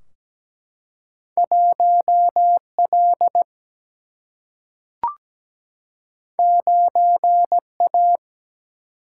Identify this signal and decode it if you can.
Morse code, heard as a keyed tone, about 17 wpm, 700 Hz. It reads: 1L 9A
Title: Single-Letter Single-Number 17wpm